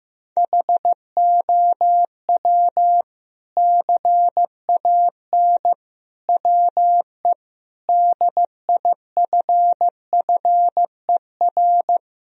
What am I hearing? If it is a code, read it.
Morse code, heard as a keyed tone, about 15 wpm, 700 Hz. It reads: HOW CAN WE DIFFER
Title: Sentences from Top 1000 Words - Review Part 1 - 15wpm